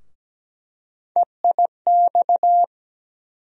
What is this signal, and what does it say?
Morse code, heard as a keyed tone, about 17 wpm, 700 Hz. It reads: EIX
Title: USA Call Sign Postfixes - 3 Letters 17wpm